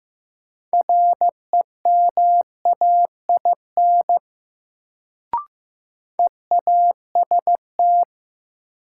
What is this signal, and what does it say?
Morse code, heard as a keyed tone, about 15 wpm, 700 Hz. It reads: REMAIN EAST
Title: Sentences from Top 1200 Words - New Word Drill Alt - 15wpm